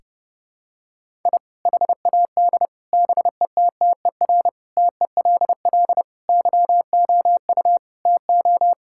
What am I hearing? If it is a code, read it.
Morse code, heard as a keyed tone, about 30 wpm, 700 Hz. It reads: I HAD BETTER TELL YOU TO
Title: Sentences from Top 1000 Words - Review Part 6 - 30wpm